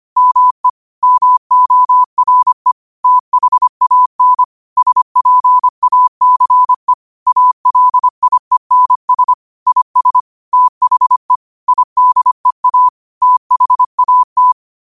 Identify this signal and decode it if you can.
Morse code, heard as a keyed tone, about 25 wpm, 1000 Hz. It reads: ME MORE THAN SPACE ALIENS IS THE IDEA THAT